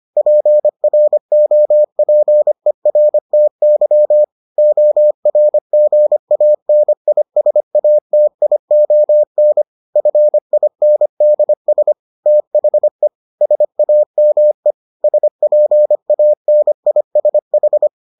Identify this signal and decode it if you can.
Morse code, heard as a keyed tone, about 25 wpm, 600 Hz. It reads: PROPERTY ORGANISATION FINDS THE SAME SPANISH